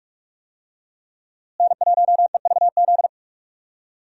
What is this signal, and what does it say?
Morse code, heard as a keyed tone, about 46 wpm, 700 Hz. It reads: N1EVZ
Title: USA Call Signs - 5 Characters - Repeated 3x - 45wpm